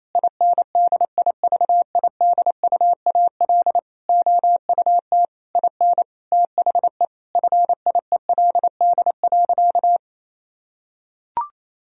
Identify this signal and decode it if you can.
Morse code, heard as a keyed tone, about 28 wpm, 700 Hz. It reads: INDIVIDUAL OUT IN THE FIELD.
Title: Sentences from Top 1000 Words - Review Part 4 - 28wpm